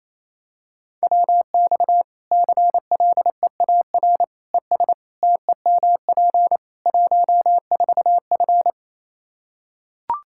Morse code, heard as a keyed tone, about 28 wpm, 700 Hz. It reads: WX CLEAR ES TEMP 14F
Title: K1USN SST QSO Practice 28wpm